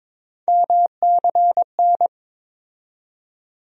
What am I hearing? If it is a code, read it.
Morse code, heard as a keyed tone, about 22 wpm, 700 Hz. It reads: MCN
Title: USA Call Sign Postfixes - 3 Letters 22wpm